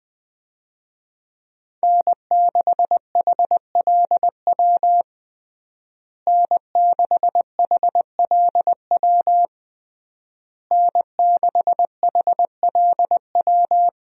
Morse code, heard as a keyed tone, about 21 wpm, 700 Hz. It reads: N6HLW N6HLW N6HLW
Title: USA Call Signs - 5 Characters - Repeated 3x - 20wpm